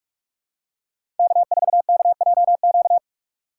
Morse code, heard as a keyed tone, about 46 wpm, 700 Hz. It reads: K4KJQ
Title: USA Call Signs - 3x Character Spacing - 45wpm